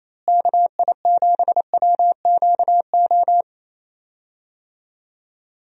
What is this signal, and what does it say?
Morse code, heard as a keyed tone, about 27 wpm, 700 Hz. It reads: KI7WQO
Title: USA Call Signs - 6 Characters - Repeated 3x - 28wpm